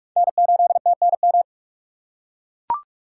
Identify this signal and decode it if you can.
Morse code, heard as a keyed tone, about 44 wpm, 700 Hz. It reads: N8TNM